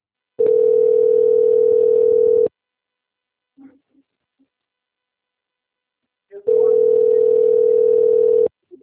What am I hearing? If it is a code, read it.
Ringback tone (North America), 440 + 480 Hz.
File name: out- - - - - .66660